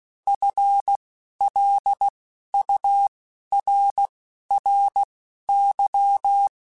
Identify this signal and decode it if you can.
Morse code, heard as a keyed tone, about 16 wpm, spaced as for 11 wpm, 800 Hz. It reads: FLURRY